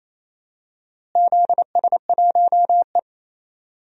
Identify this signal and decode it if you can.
Morse code, heard as a keyed tone, about 28 wpm, 700 Hz. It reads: ZS1E
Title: DXCC-100 Call Signs - 4 Characters - Repeated 3x - 28wpm